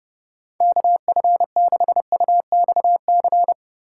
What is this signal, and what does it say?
Morse code, heard as a keyed tone, about 31 wpm, 700 Hz. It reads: KF6UXC